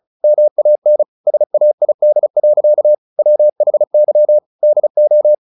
Morse code, heard as a keyed tone, about 35 wpm, 600 Hz. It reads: MAN SAID. WHY DO
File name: the old man and the sea / the old man and the sea 1 35wpm